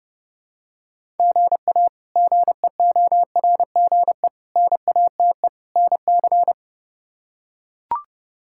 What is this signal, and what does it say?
Morse code, heard as a keyed tone, about 30 wpm, 700 Hz. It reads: GA GEORGE NATE NC
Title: K1USN SST QSO Practice - 30wpm